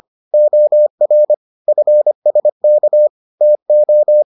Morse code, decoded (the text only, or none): OR FSK TO